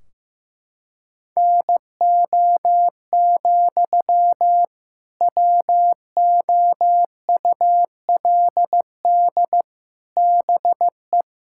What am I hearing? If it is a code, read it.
Morse code, heard as a keyed tone, about 14 wpm, 700 Hz. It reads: NO, WOULD BE
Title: Sentences from Top 1600 Words 15wpm